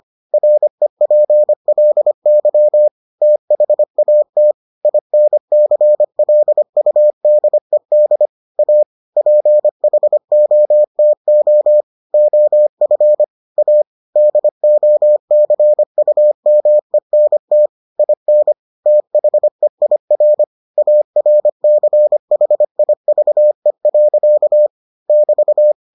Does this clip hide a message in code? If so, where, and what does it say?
Yes: Morse code from 0.3 to 25.7 s, heard as a keyed tone, about 25 wpm, 600 Hz. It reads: REPLY THAT INCLUDED A PHOTO OF A DOCUMENT IN THEIR ARCHIVE. <BT>